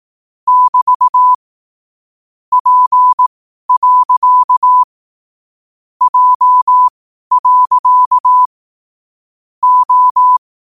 Morse code, heard as a keyed tone, about 18 wpm, spaced as for 12 wpm, 1000 Hz. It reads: <BT> P. J. O
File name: qotdfiles / 12wpm